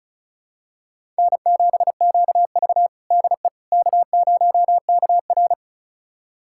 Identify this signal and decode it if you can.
Morse code, heard as a keyed tone, about 36 wpm, 700 Hz. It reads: N7QV DE K0KR
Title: USA Callsign DE Callsign - 4 Characters - 35wpm